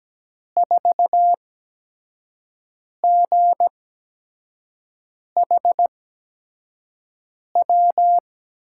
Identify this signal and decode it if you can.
Morse code, heard as a keyed tone, about 17 wpm, spaced as for 5 wpm, 700 Hz. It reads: 4GHW
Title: USA Call Signs - 6 Characters - 8x Character Spacing - 17wpm